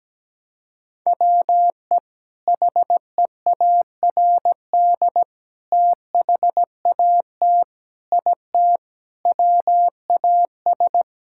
Morse code, heard as a keyed tone, about 17 wpm, 700 Hz. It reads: WE HEARD THAT IT WAS
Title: Sentences from Top 1000 Words - Review Part 6 - 17wpm